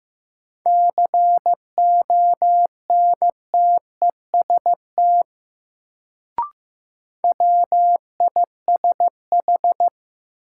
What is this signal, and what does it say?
Morse code, heard as a keyed tone, about 15 wpm, 700 Hz. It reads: CONTEST WISH
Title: Sentences from Top 1600 Words - New Word Drill - 15wpm